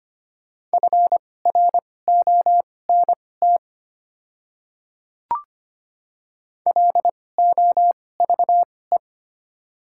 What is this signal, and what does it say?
Morse code, heard as a keyed tone, about 25 wpm, spaced as for 18 wpm, 700 Hz. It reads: FRONT LOVE
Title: 2x Character Spacing - Top 500 Words - 25wpm